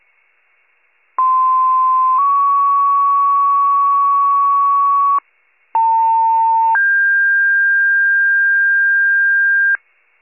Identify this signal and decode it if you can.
Two-tone page, heard as a one-second tone, then a three-second tone, about 1040 then 1120 Hz, about 900 then 1600 Hz. It is heard from 1.2 to 9.8 s.